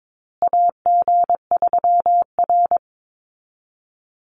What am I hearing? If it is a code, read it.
Morse code, heard as a keyed tone, about 22 wpm, 700 Hz. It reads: AG3R
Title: Call Signs Repeated 3x - 22wpm